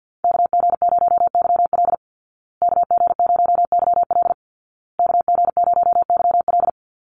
Morse code, heard as a keyed tone, about 50 wpm, 700 Hz. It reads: KG0YL KG0YL KG0YL
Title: Call Signs Repeated 2x - 50wpm